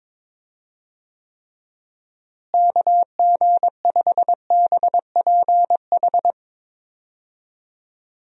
Morse code, heard as a keyed tone, about 22 wpm, 700 Hz. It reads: KG5BPH
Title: USA Call Signs - 6 Characters - Repeated 3x - 22wpm